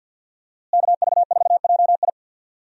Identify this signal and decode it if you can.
Morse code, heard as a keyed tone, about 48 wpm, 700 Hz. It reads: KV4JI
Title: USA Call Signs - 5 Characters - Repeated 3x - 50wpm